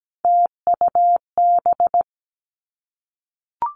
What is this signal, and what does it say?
Morse code, heard as a keyed tone, about 17 wpm, 700 Hz. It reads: TUB
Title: Three Challenging Letter-Number Combinations 17wpm